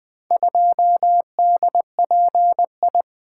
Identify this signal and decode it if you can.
Morse code, heard as a keyed tone, about 20 wpm, 700 Hz. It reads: 2DPI
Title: Four Letter-Number Combinations - 20wpm